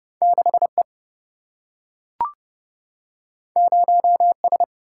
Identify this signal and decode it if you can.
Morse code, heard as a keyed tone, about 31 wpm, 700 Hz. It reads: 6E 0S
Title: Single-Letter Single-Number 30wpm